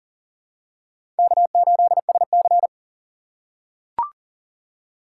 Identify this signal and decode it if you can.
Morse code, heard as a keyed tone, about 40 wpm, 700 Hz. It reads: K8SC
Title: USA Call Signs - 4 Characters - 6x Character Spacing - 40wpm